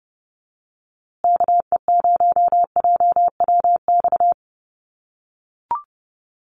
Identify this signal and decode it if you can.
Morse code, heard as a keyed tone, about 30 wpm, 700 Hz. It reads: KE0JWX
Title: USA Call Signs Repeated 2x 30wpm